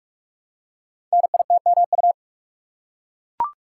Morse code, heard as a keyed tone, about 44 wpm, 700 Hz. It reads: NITMU
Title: Code Groups - 5 Letter:Numbers - 45wpm